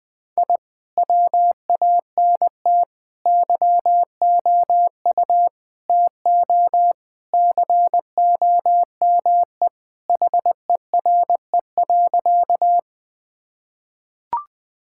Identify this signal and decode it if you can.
Morse code, heard as a keyed tone, about 20 wpm, 700 Hz. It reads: I WANT YOU TO COME HERE.
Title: Sentences with 2-Word Phrases - 20wpm